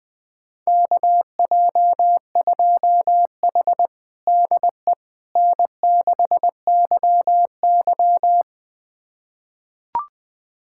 Morse code, heard as a keyed tone, about 20 wpm, 700 Hz. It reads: KJ2H DE N6YY
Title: USA Callsign DE Callsign - 4 Characters - 20wpm